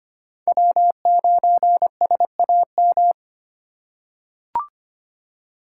Morse code, heard as a keyed tone, about 25 wpm, 700 Hz. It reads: W9SAM
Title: USA Call Signs - 5 Characters - 2x Character Spacing - 25wpm